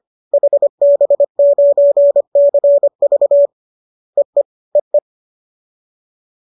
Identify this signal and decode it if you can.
Morse code, heard as a keyed tone, about 25 wpm, 570 Hz. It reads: HB9CV EE EE
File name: qso210 25